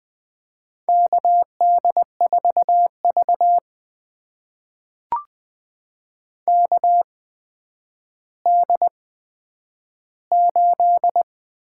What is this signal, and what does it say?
Morse code, heard as a keyed tone, about 21 wpm, spaced as for 10 wpm, 700 Hz. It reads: KD4V KD8
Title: USA Call Signs - 8x Character Spacing - 20wpm